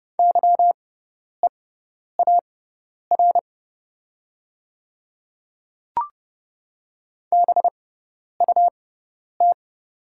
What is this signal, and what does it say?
Morse code, heard as a keyed tone, about 30 wpm, spaced as for 10 wpm, 690 Hz. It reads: YEAR BUT
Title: 6x Character Spacing - Top 200 Words - 30wpm